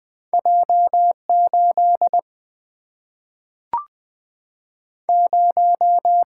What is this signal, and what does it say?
Morse code, heard as a keyed tone, about 20 wpm, 700 Hz. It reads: J8 0
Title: Single-Letter Single-Number 20wpm